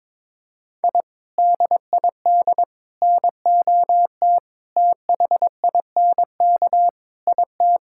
Morse code, heard as a keyed tone, about 22 wpm, 700 Hz. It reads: I DID NOT THINK IT
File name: Sentences with 4-Word Phrases - 22wpm